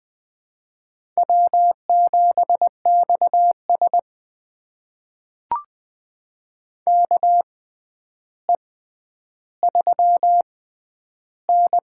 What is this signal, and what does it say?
Morse code, heard as a keyed tone, about 21 wpm, spaced as for 10 wpm, 700 Hz. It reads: W7XS KE3N
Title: USA Call Signs - 6x Character Spacing - 20wpm